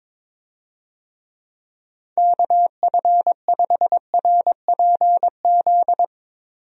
Morse code, heard as a keyed tone, about 21 wpm, 700 Hz. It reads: KF5RPZ
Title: USA Call Signs - 6 Characters - Repeated 2x - 22wpm